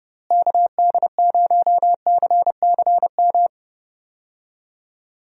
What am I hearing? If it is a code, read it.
Morse code, heard as a keyed tone, about 31 wpm, 700 Hz. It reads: KD0CCM
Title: USA Call Signs - 6 Characters - Repeated 2x - 30wpm